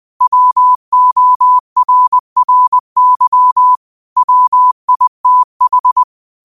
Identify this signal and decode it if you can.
Morse code, heard as a keyed tone, about 20 wpm, 1000 Hz. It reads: WORRY WITH